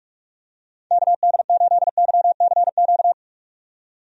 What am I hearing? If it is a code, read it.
Morse code, heard as a keyed tone, about 42 wpm, 700 Hz. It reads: KD8YCQ
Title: USA Call Signs - 6 Characters - 45wpm